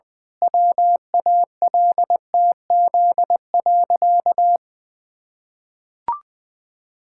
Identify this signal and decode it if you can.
Morse code, heard as a keyed tone, about 20 wpm, 700 Hz. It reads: WALTZ.
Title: Pangram Challenge v2 - 20wpm